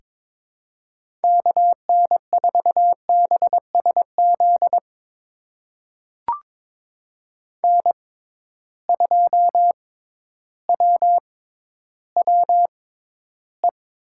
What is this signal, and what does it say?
Morse code, heard as a keyed tone, about 22 wpm, spaced as for 12 wpm, 700 Hz. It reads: KN4BSZ N2WWE
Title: USA Call Signs - 6x Character Spacing - 22wpm